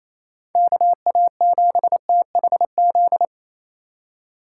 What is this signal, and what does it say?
Morse code, heard as a keyed tone, about 36 wpm, 700 Hz. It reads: KA7THZ